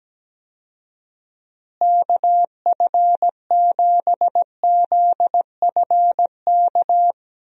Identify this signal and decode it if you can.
Morse code, heard as a keyed tone, about 17 wpm, 700 Hz. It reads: KF7ZFK
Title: USA Call Signs - 6 Characters - Repeated 3x - 17wpm